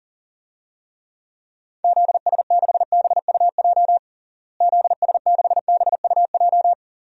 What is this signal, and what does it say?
Morse code, heard as a keyed tone, about 40 wpm, 700 Hz. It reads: ZS6BUJ ZS6BUJ
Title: DXCC-100 Call Signs - 6 Characters - Repeated 3x - 40wpm